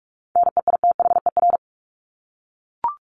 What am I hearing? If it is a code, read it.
Morse code, heard as a keyed tone, about 44 wpm, 700 Hz. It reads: NEITHER
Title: Top 1501-1750 Words 45wpm